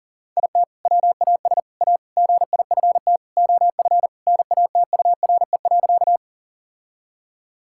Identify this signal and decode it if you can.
Morse code, heard as a keyed tone, about 40 wpm, 700 Hz. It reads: IT WAS A GIFT OF NATURE.
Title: Sentences from Top 1000 Words - Review Part 6 - 40wpm